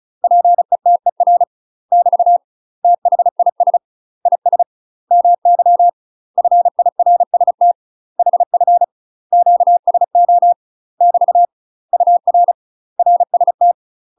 Morse code, heard as a keyed tone, about 34 wpm, 700 Hz. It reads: PETER <BT> THIS IS MY FIRST HF QSO <BT> UR RST
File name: UK MASTER QSOs TRAINING / 35WPM / UK MASTER QSOs 0025